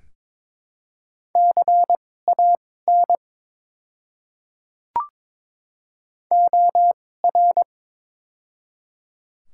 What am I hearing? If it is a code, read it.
Morse code, heard as a keyed tone, about 22 wpm, spaced as for 16 wpm, 700 Hz. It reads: CAN OR
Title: 2x Character Spacing - Top 100 Words - 22wpm